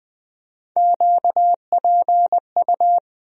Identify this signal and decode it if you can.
Morse code, heard as a keyed tone, about 20 wpm, 700 Hz. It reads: QPU
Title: USA Call Sign Postfixes - 3 Letters 20wpm